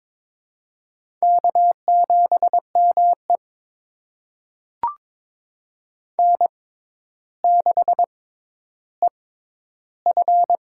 Morse code, heard as a keyed tone, about 22 wpm, spaced as for 11 wpm, 700 Hz. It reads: K7ME N6EF
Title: USA Call Signs - 4 Characters - 6x Character Spacing - 22wpm